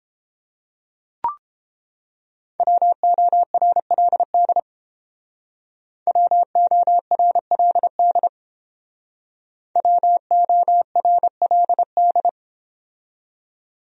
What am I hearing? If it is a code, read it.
Morse code, heard as a keyed tone, about 30 wpm, 700 Hz. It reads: WORLD WORLD WORLD